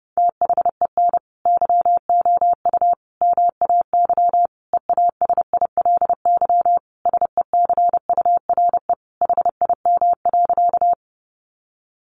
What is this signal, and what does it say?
Morse code, heard as a keyed tone, about 28 wpm, 700 Hz. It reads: THEN YOU MAY EASILY SECURE HIM.
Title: Sentences from Top 1200 Words 30wpm